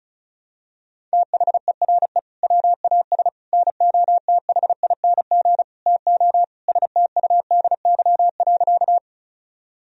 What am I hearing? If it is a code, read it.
Morse code, heard as a keyed tone, about 36 wpm, 700 Hz. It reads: THERE WAS NOTHING TO STUDY.
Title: Sentences from Top 1000 Words - Review Part 1 - 35wpm